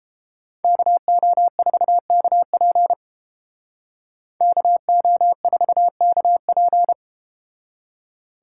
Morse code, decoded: KO4KP KO4KP

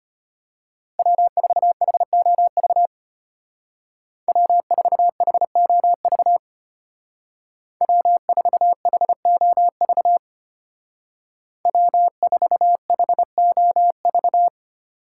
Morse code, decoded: W4HOV W4HOV W4HOV W4HOV

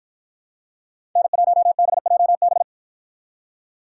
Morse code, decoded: N1BJB

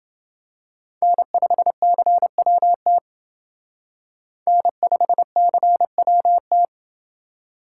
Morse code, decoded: N5CWT N5CWT

